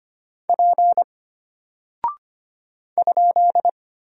Morse code, heard as a keyed tone, about 25 wpm, 700 Hz. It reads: P ?
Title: Single Character (Letter:Number:Punctuation:Prosign) Alt 25wpm